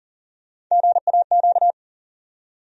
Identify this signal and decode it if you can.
Morse code, heard as a keyed tone, about 40 wpm, 700 Hz. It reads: GAQ
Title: USA Call Sign Postfixes - 3 Letters 40wpm